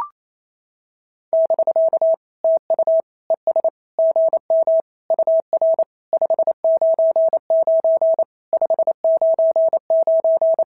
Morse code, heard as a keyed tone, about 28 wpm, 650 Hz. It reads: <BK> TU ES GM UR 599 599
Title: SOTA QSO Practice 28wpm